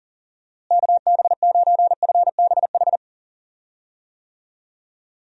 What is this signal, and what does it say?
Morse code, heard as a keyed tone, about 40 wpm, 700 Hz. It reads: KB9FBH
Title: USA Call Signs - 6 Characters - Repeated 2x - 40wpm